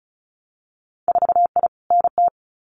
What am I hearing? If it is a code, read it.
Morse code, heard as a keyed tone, about 36 wpm, 700 Hz. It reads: 4I NT
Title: Winter Field Day Exchanges Encore 35wpm